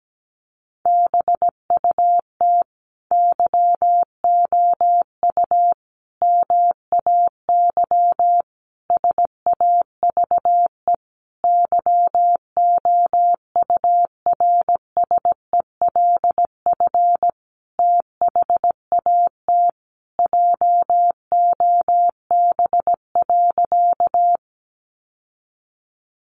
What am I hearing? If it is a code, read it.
Morse code, heard as a keyed tone, about 17 wpm, 700 Hz. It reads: BUT YOU MAY SAVE YOURSELF THAT JOB.
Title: Sentences from Top 1100 Words 17wpm